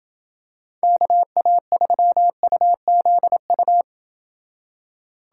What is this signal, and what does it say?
Morse code, heard as a keyed tone, about 28 wpm, 700 Hz. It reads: KA3UZU